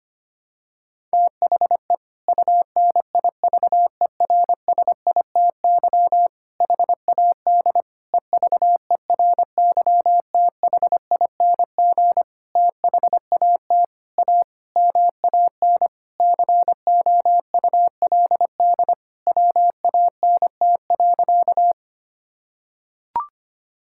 Morse code, heard as a keyed tone, about 25 wpm, 700 Hz. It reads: THE UNIVERSITY HAD EVERYTHING THAT A MAN COULD WANT.
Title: Sentences from Top 1000 Words - Review Part 5 - 25wpm